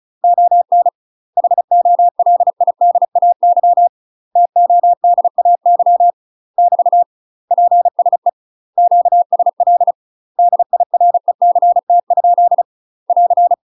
Morse code, heard as a keyed tone, about 36 wpm, 700 Hz. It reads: ON HOLIDAY TODAY <BT> PSE QSL DIRECT? <AR>
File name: UK MASTER QSOs TRAINING / 35WPM / UK MASTER QSOs 0031